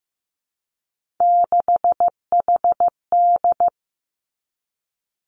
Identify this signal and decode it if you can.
Morse code, heard as a keyed tone, about 15 wpm, 700 Hz. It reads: 6HD
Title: Three Challenging Letter-Number Combinations 15wpm